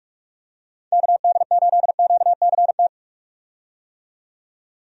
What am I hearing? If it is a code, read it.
Morse code, heard as a keyed tone, about 46 wpm, 700 Hz. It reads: KD8QCT
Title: USA Call Signs - 6 Characters - 45wpm